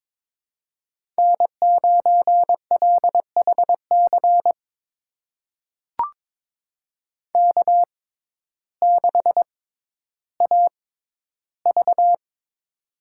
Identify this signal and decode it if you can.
Morse code, heard as a keyed tone, about 22 wpm, spaced as for 12 wpm, 700 Hz. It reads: N9LHC K6AV
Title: USA Call Signs - 6x Character Spacing - 22wpm